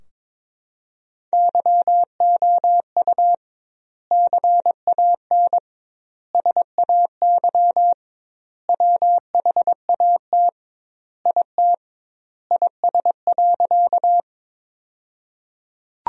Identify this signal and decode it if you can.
Morse code, heard as a keyed tone, about 21 wpm, 700 Hz. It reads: YOU CAN SAY WHAT IT IS.
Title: 2x Word-Spacing - Sentences from Top 200 Words - 22wpm